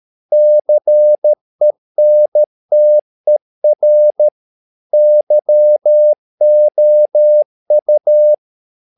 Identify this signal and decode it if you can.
Morse code, heard as a keyed tone, about 13 wpm, 600 Hz. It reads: CENTER YOU